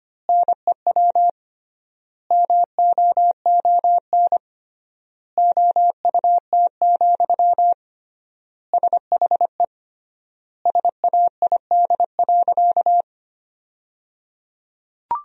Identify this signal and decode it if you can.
Morse code, heard as a keyed tone, about 25 wpm, 700 Hz. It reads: NEW MOON OUT, SHE SAID.
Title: 3x Word-Spacing - Sentences from Top 500 Words - 25wpm